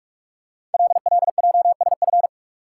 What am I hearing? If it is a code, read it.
Morse code, heard as a keyed tone, about 48 wpm, 700 Hz. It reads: LLJSF
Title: Code Groups - 5 Letters - 45wpm